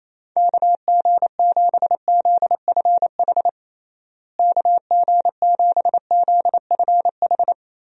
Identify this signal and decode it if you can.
Morse code, heard as a keyed tone, about 28 wpm, 700 Hz. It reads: KG7ZFH KG7ZFH
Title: USA Call Signs - 6 Characters - Repeated 2x - 28wpm